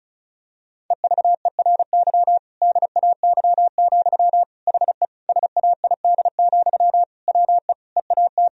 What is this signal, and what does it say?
Morse code, heard as a keyed tone, about 36 wpm, 700 Hz. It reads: EVERY DAY, HE SAID, WE EAT
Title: Sentences from Top 1000 Words - Review Part 1 - 35wpm